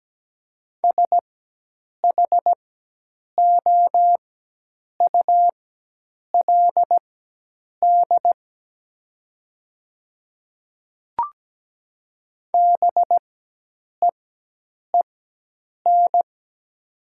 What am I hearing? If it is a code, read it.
Morse code, heard as a keyed tone, about 17 wpm, spaced as for 8 wpm, 700 Hz. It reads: SHOULD BEEN
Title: 4x Character Spacing - Top 300 Words - 17wpm